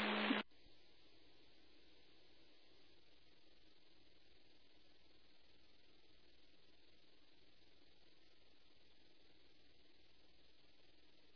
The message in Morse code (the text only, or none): none